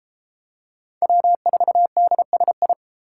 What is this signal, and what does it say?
Morse code, heard as a keyed tone, about 32 wpm, 700 Hz. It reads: W4DSI